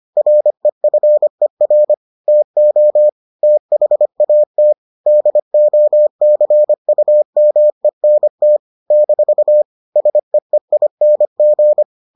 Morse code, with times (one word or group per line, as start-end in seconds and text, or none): 0.2-1.9 REFER
2.3-3.1 TO
3.4-4.7 THAT
5.1-8.6 DOCUMENT
8.9-9.6 -
10.0-11.8 SEEING